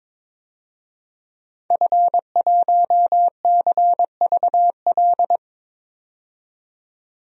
Morse code, heard as a keyed tone, about 22 wpm, 700 Hz. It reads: F1CVL
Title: DXCC-100 Call Signs - 5 Characters - 22wpm